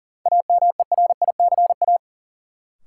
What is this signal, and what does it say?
Morse code, heard as a keyed tone, about 40 wpm, 700 Hz. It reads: AMERICA
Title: Sets of 5 Words - New Word Drill - 40wpm